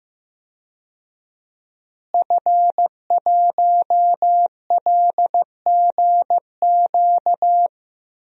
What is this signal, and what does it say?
Morse code, heard as a keyed tone, about 15 wpm, 700 Hz. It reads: F1LGQ
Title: DXCC-100 Call Signs - 5 Characters - Repeated 2x - 15wpm